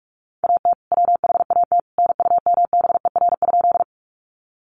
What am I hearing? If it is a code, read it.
Morse code, heard as a keyed tone, about 46 wpm, 700 Hz. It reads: AT WHAT NUMBER?
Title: Sentences from Top 100 Words 45wpm